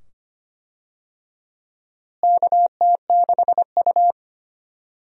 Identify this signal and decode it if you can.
Morse code, heard as a keyed tone, about 25 wpm, 700 Hz. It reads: KT6U
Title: USA Call Signs - 4 Characters - Repeated 2x - 25wpm